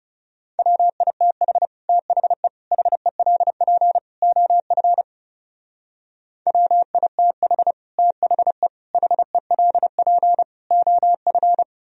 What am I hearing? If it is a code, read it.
Morse code, heard as a keyed tone, about 36 wpm, 700 Hz. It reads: WITH THE HELP OF WITH THE HELP OF
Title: Speed-Racing - Sets of 4 Words 30wpm